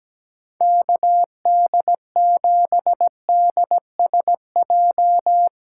Morse code, heard as a keyed tone, about 17 wpm, 700 Hz. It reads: KD7DSJ
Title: USA Callsign DE Callsign 17wpm